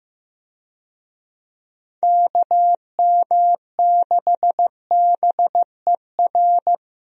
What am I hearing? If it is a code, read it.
Morse code, heard as a keyed tone, about 16 wpm, 700 Hz. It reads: KM6BER
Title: USA Call Signs - 6 Characters - Repeated 3x - 15wpm